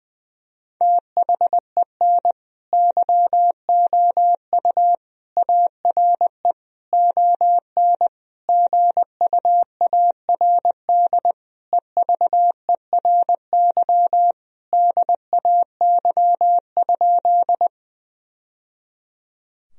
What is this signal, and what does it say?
Morse code, heard as a keyed tone, about 20 wpm, 700 Hz. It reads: THEN YOU ARE ON GUARD EVERY DAY?
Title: Sentences from Top 1000 Words - Review Part 4 - 20wpm